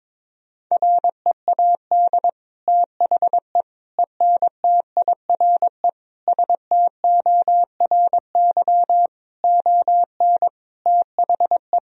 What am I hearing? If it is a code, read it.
Morse code, heard as a keyed tone, about 22 wpm, 700 Hz. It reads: READ THE ENTIRE STORY ON THE
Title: Sentences from Top 1000 Words - Review Part 4 - 22wpm